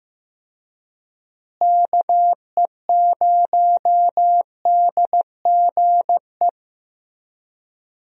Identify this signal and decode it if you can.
Morse code, heard as a keyed tone, about 15 wpm, 700 Hz. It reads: KE0DGE
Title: USA Call Signs - 6 Characters - Repeated 2x - 15wpm